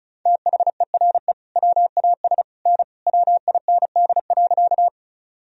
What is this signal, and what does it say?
Morse code, heard as a keyed tone, about 36 wpm, 700 Hz. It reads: THERE WAS N WIND.
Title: Sentences from Top 400 Words and Abbreviations 35wpm